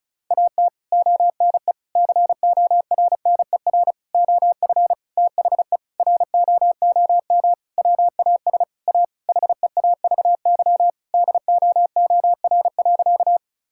Morse code, heard as a keyed tone, about 36 wpm, 700 Hz. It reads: AT ONE CORNER OF THE ROOM WAS A HEAVY DOOR.
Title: Sentences from Top 1600 Words 35wpm